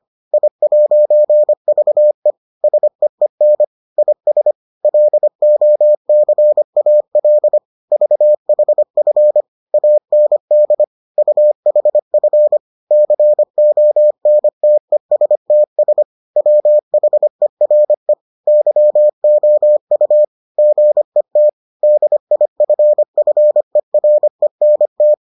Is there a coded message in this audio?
Yes: Morse code from 0.3 to 25.1 s, heard as a keyed tone, about 25 wpm, 600 Hz. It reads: I'VE SEEN IS LOCAL VHF AND UHF CONTESTS WHERE YOU GET DIFFERENT